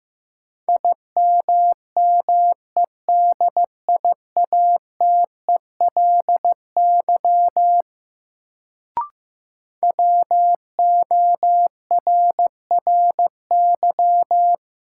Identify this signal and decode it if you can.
Morse code, heard as a keyed tone, about 15 wpm, 700 Hz. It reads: IMMEDIATELY WORRY
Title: Sentences from Top 1100 Words - New Word Drill - 15wpm